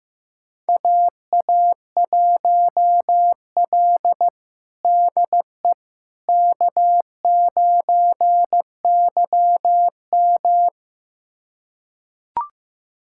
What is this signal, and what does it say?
Morse code, heard as a keyed tone, about 15 wpm, 700 Hz. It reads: AA1L DE K9YM
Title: USA Callsign DE Callsign - 4 Characters - 15wpm